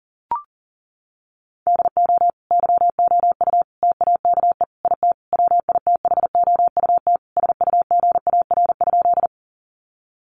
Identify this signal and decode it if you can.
Morse code, heard as a keyed tone, about 40 wpm, 700 Hz. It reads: DO YOU TAKE IT WITHOUT SUGAR?
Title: Sentences from Top 1200 Words 40wpm